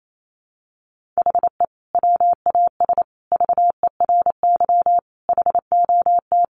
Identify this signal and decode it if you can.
Morse code, heard as a keyed tone, about 28 wpm, 700 Hz. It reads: HE WAS VERY HOT
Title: Sentences from Top 1400 Words 28wpm